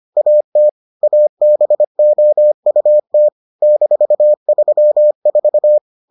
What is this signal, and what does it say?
Morse code, heard as a keyed tone, about 25 wpm, 600 Hz. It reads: AT ABOUT -34